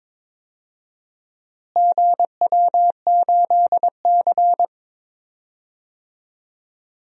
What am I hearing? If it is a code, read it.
Morse code, heard as a keyed tone, about 21 wpm, 700 Hz. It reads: GW8C
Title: DXCC-100 Call Signs - 4 Characters - Repeated 2x - 22wpm